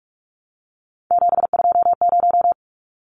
Morse code, heard as a keyed tone, about 44 wpm, 700 Hz. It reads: 720
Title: Numbers - 3-Digit 45wpm